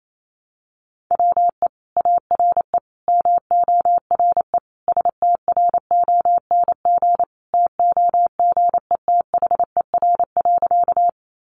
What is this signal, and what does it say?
Morse code, heard as a keyed tone, about 28 wpm, 700 Hz. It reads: WE ARE MORE STRONG TOGETHER.
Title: Sentences from Top 500 Words 28wpm